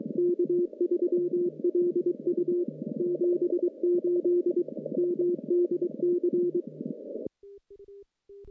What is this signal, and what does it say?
Morse code, heard as a keyed tone, about 22 wpm, 350 Hz. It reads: K3LU 78 MDC
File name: contests / CW-SS-k5tr- -1200